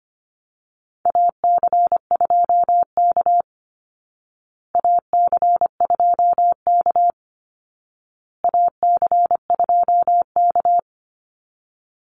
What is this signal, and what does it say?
Morse code, heard as a keyed tone, about 25 wpm, 700 Hz. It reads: AC2K AC2K AC2K
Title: Call Signs Repeated 2x - 25wpm